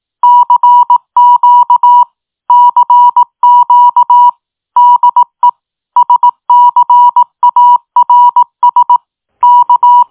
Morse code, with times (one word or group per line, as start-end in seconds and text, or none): 0.2-2.0 CQ
2.5-4.3 CQ
4.8-5.5 DE
6.0-9.0 SCARS
9.4-10.0 K